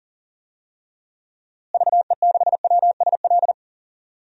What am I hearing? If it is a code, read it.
Morse code, heard as a keyed tone, about 37 wpm, 700 Hz. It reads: VE6WSL
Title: DXCC-100 Call Signs - 6 Characters - Repeated 2x - 40wpm